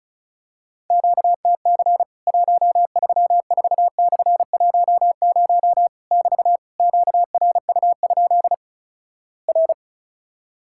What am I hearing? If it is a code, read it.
Morse code, heard as a keyed tone, about 36 wpm, 700 Hz. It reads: QTC 134/10 <BT> QRU? R
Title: Worked All Europe DX Contest - QTC Exchange - 35wpm